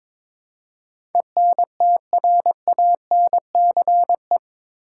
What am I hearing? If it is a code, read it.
Morse code, heard as a keyed tone, about 22 wpm, 700 Hz. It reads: ENTRANCE
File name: practice / 22wpm / Eight Letter Words - 22wpm